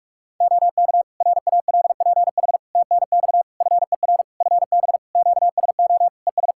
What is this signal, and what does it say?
Morse code, heard as a keyed tone, about 46 wpm, 700 Hz. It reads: OK RALPH TNX FER FB QSO ES
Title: Practice QSO - Exchange 4 - Ending - 45wpm